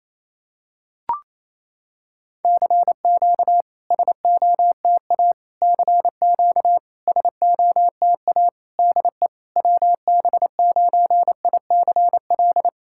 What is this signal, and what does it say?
Morse code, heard as a keyed tone, about 28 wpm, 700 Hz. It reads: CQ SOTA CQ SOTA DE WB9ICL
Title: SOTA QSO Practice 28wpm